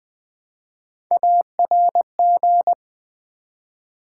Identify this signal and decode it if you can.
Morse code, heard as a keyed tone, about 20 wpm, 700 Hz. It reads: ARG
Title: Build A Word - 1 Letter At A Time - 5-Letter Words - 20wpm